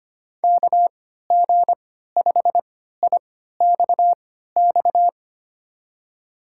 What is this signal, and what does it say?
Morse code, heard as a keyed tone, about 25 wpm, spaced as for 14 wpm, 700 Hz. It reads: KG5IXX
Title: USA Call Signs - 6 Characters - 3x Character Spacing - 25wpm